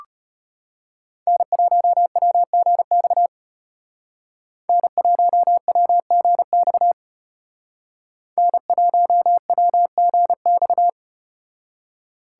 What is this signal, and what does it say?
Morse code, heard as a keyed tone, about 34 wpm, 700 Hz. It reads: N1WGX N1WGX N1WGX